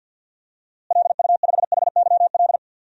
Morse code, heard as a keyed tone, about 47 wpm, 700 Hz. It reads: LU5HYL
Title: DXCC-100 Call Signs - 6 Characters - Repeated 2x - 50wpm